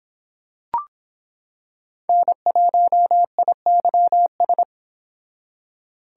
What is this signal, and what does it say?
Morse code, heard as a keyed tone, about 25 wpm, 700 Hz. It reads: N1IYS